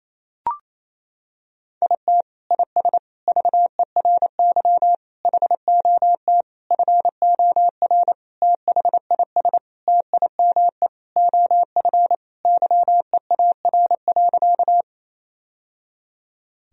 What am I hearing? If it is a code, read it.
Morse code, heard as a keyed tone, about 28 wpm, 700 Hz. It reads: IT IS VERY HOT FOR THIS TIME OF YEAR.
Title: Sentences from Top 1000 Words - Review Part 3 - 28wpm